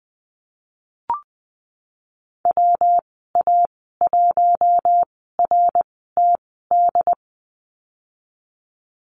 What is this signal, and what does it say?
Morse code, heard as a keyed tone, about 20 wpm, spaced as for 15 wpm, 700 Hz. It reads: WA1RTD
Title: 161 Lesson 26g - Letter g - Call Signs 20wpm